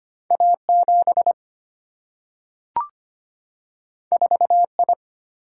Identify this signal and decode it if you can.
Morse code, heard as a keyed tone, about 25 wpm, 700 Hz. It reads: A 7 4 I
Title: Single-Letter Single-Number 25wpm